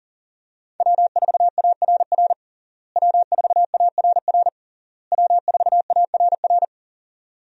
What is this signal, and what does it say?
Morse code, heard as a keyed tone, about 40 wpm, 700 Hz. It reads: W4ARR W4ARR W4ARR
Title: USA Call Signs - 5 Characters - Repeated 3x - 40wpm